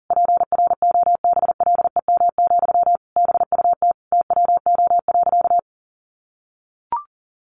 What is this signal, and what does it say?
Morse code, heard as a keyed tone, about 40 wpm, 700 Hz. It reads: PROBLEM, BUT TWO.
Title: Sentences from Top 1400 Words 40wpm